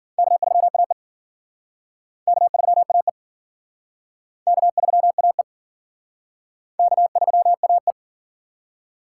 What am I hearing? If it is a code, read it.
Morse code, heard as a keyed tone, about 51 wpm, 700 Hz. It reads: K3AE K3AE K3AE K3AE